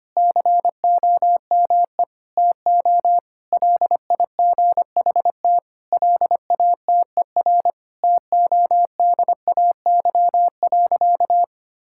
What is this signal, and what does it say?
Morse code, heard as a keyed tone, about 25 wpm, 700 Hz. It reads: COME TO LIGHT LATER TODAY.
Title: Sentences from Top 1000 Words - Review Part 6 - 25wpm